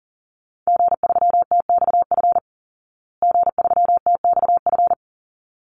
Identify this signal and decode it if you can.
Morse code, heard as a keyed tone, about 40 wpm, 700 Hz. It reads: G3TXF G3TXF
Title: Call Signs Repeated 3x - 40wpm